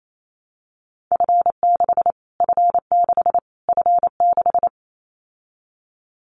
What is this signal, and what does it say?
Morse code, heard as a keyed tone, about 28 wpm, 700 Hz. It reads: F6 F6 F6
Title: Callsign Prefixes for Top 25 DXCC - Repeated 3x - 28wpm